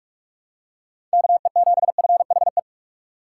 Morse code, heard as a keyed tone, about 44 wpm, 690 Hz. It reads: KE7FHE
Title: USA Call Signs - 6 Characters - 3x Character Spacing - 45wpm